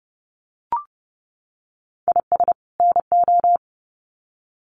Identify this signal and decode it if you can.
Morse code, heard as a keyed tone, about 30 wpm, 700 Hz. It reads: IS NO
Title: Sets of 2 Words 30wpm